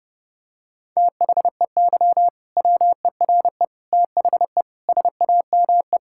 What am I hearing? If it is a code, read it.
Morse code, heard as a keyed tone, about 30 wpm, 700 Hz. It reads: THEY WERE THE SAME
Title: Sentences from Top 1600 Words 30wpm